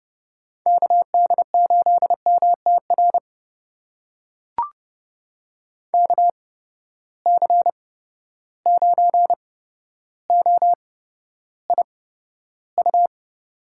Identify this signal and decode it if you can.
Morse code, heard as a keyed tone, about 30 wpm, spaced as for 13 wpm, 700 Hz. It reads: KD8MTR KC9OIU
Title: USA Call Signs - 6 Characters - 8x Character Spacing - 30wpm